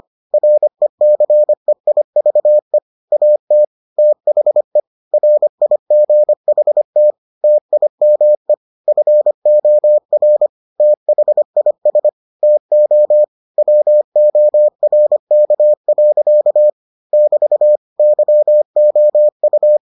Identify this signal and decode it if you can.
Morse code, heard as a keyed tone, about 25 wpm, 600 Hz. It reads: RECEIVE AT THE RIGHT TIME FOR THIS TO WORK. <BT> YOU